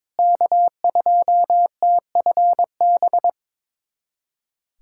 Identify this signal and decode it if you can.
Morse code, heard as a keyed tone, about 22 wpm, 700 Hz. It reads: K2TFB
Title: USA Callsign DE Callsign - 5 Characters - 22wpm